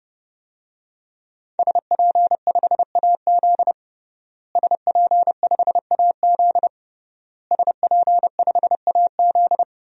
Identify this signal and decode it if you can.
Morse code, heard as a keyed tone, about 29 wpm, 700 Hz. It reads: SP5AZ SP5AZ SP5AZ
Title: DXCC-100 Call Signs - 5 Characters - Repeated 3x - 30wpm